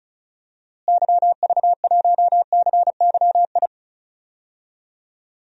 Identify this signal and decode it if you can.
Morse code, heard as a keyed tone, about 34 wpm, 700 Hz. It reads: YV1CYI
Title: DXCC-100 Call Signs - 6 Characters - Repeated 2x - 35wpm